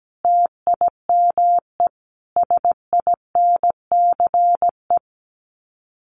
Morse code, heard as a keyed tone, about 17 wpm, 700 Hz. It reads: TIME SINCE
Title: Sets of 4 Words 17wpm